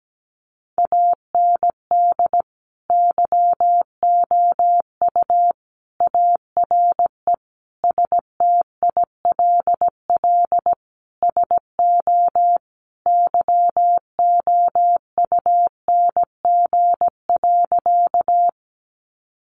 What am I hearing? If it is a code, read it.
Morse code, heard as a keyed tone, about 17 wpm, 700 Hz. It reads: AND YOU ARE STILL SO YOUNG.
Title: Sentences from Top 400 Words 17wpm